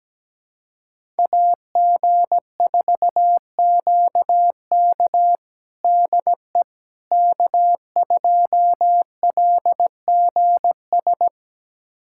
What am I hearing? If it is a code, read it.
Morse code, heard as a keyed tone, about 17 wpm, 700 Hz. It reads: AG4QK DE K2LGS
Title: USA Callsign DE Callsign - 5 Characters - 17wpm